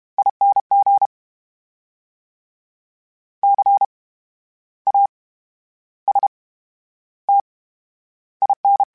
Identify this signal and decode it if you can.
Morse code, heard as a keyed tone, about 32 wpm, spaced as for 12 wpm, 800 Hz. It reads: ING CASTIN
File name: Special INGs / Special INGs-mp3s / VerySlowToFaster Ing @30